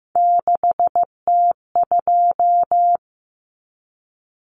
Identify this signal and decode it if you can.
Morse code, heard as a keyed tone, about 15 wpm, 700 Hz. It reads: 6T2
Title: Common Cut Numbers - 3-Digit 15wpm